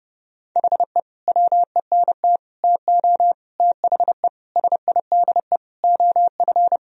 Morse code, heard as a keyed tone, about 30 wpm, 700 Hz. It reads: HE WENT TO THE SIDE OF
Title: Sentences from Top 1000 Words - Review Part 1 - 30wpm